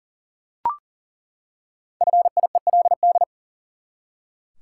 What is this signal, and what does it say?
Morse code, heard as a keyed tone, about 40 wpm, 700 Hz. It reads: FIELD
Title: Five Letter Words - 40wpm